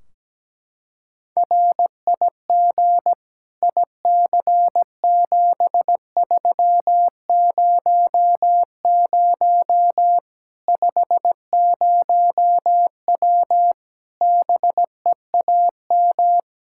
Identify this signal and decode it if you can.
Morse code, heard as a keyed tone, about 17 wpm, 700 Hz. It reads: RIG IC7300 50W BEAM
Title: Practice QSO - Exchange 2 - Info - 17wpm